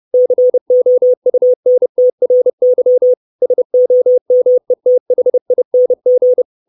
Morse code, heard as a keyed tone, about 30 wpm, 500 Hz. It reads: COUNTRY SOMETHING
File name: 632 Words 30WPM